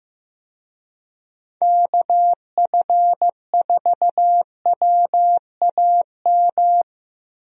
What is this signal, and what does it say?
Morse code, heard as a keyed tone, about 15 wpm, 700 Hz. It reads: KF4WAM
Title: USA Call Signs - 6 Characters - Repeated 2x - 15wpm